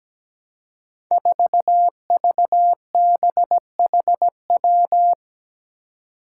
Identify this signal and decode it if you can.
Morse code, heard as a keyed tone, about 17 wpm, 700 Hz. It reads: 4VBHW
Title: Five Challenging Letter-Number Combinations 17wpm